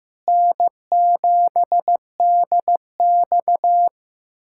Morse code, heard as a keyed tone, about 15 wpm, 700 Hz. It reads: N7DX
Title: USA Callsign DE Callsign - 4 Characters - 15wpm